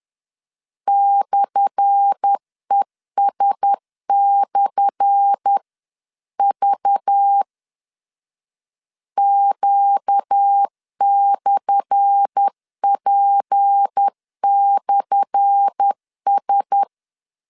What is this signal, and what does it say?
Morse code, heard as a keyed tone, about 10 wpm, 800 Hz. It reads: /ES/V Q/P/S
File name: LEZIONI / 10 SIGNS / 10A SLASH 10wpm 2808